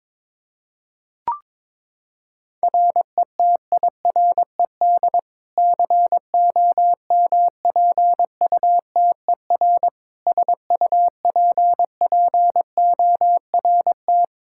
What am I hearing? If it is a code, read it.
Morse code, heard as a keyed tone, about 22 wpm, 700 Hz. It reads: RETIRED COMPUTER SUPPORT
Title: Practice QSO - Exchange 3 - More Info - 22wpm